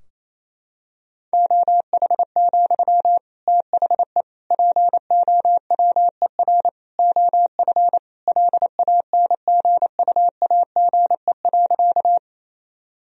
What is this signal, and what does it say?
Morse code, heard as a keyed tone, about 28 wpm, 700 Hz. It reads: OH, THE POWER OF LANGUAGE.
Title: Sentences from Top 1000 Words - Review Part 3 - 28wpm